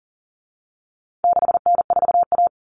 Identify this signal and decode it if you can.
Morse code, heard as a keyed tone, about 40 wpm, 700 Hz. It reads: 6N4A
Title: Common Cut Numbers - 4-Digit 40wpm